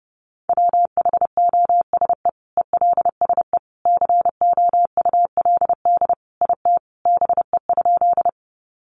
Morse code, heard as a keyed tone, about 30 wpm, 700 Hz. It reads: WHOSE ELSE COULD IT BE?
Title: Sentences from Top 1200 Words 30wpm